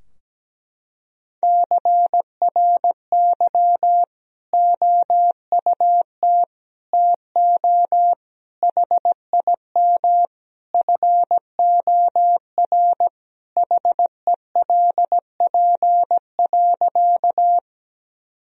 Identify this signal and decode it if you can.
Morse code, heard as a keyed tone, about 17 wpm, 700 Hz. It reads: CRY OUT TO HIM FOR HELP.
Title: Sentences from Top 1000 Words - Review Part 2 - 17wpm